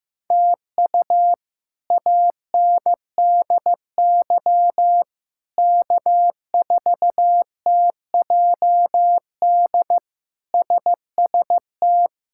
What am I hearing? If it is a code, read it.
Morse code, heard as a keyed tone, about 15 wpm, 700 Hz. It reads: TU ANDY K4TJD SST
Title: K1USN SST QSO Practice - 15wpm